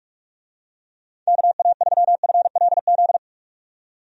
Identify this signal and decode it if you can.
Morse code, heard as a keyed tone, about 46 wpm, 700 Hz. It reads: KA3FLZ
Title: USA Call Signs - 6 Characters - 45wpm